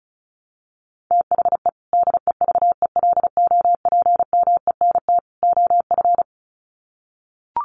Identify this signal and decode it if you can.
Morse code, heard as a keyed tone, about 34 wpm, 700 Hz. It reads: THE DEVELOPMENT OF
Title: Sets of 3 Words 35wpm